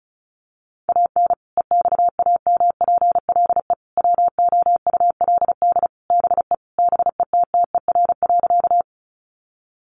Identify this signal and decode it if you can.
Morse code, heard as a keyed tone, about 34 wpm, 700 Hz. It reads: AN EXAMPLE WOULD BE BETTER.
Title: Sentences from Top 400 Words 35wpm